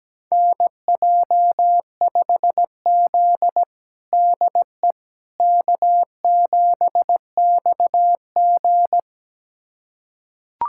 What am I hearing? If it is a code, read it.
Morse code, heard as a keyed tone, about 17 wpm, 700 Hz. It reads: NJ5Z DE K7XG
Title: USA Callsign DE Callsign - 4 Characters - 17wpm